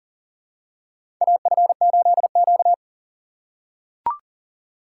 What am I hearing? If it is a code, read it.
Morse code, heard as a keyed tone, about 40 wpm, 700 Hz. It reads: AF8Q
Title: USA Call Signs - 4 Characters - 4x Character Spacing - 40wpm